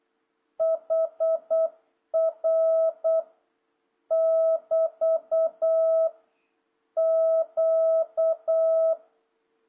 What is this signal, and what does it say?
Morse code, heard as a keyed tone, about 8 wpm, 650 Hz. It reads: HR <BT> Q
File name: jh8cbh / 2026cw